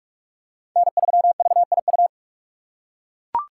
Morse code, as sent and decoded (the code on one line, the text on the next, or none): -. ...-- ...- .. ..-
N3VIU